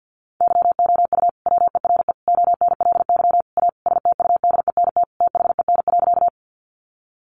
Morse code, heard as a keyed tone, about 50 wpm, 700 Hz. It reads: YOU WERE ONLY A STUDENT THEN.
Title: Sentences from Top 1000 Words 50wpm